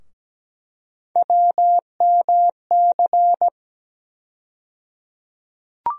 Morse code, heard as a keyed tone, about 17 wpm, 700 Hz. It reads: WMC
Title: USA Call Sign Postfixes - 3 Letters 17wpm